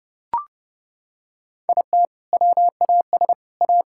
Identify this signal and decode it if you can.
Morse code, heard as a keyed tone, about 30 wpm, 700 Hz. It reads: IT WAS A
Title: Sentences from Top 1000 Words - Review Part 5 - 30wpm